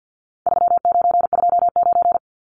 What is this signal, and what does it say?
Morse code, heard as a keyed tone, about 48 wpm, 700 Hz. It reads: 3929
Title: Numbers - 4-Digit 50wpm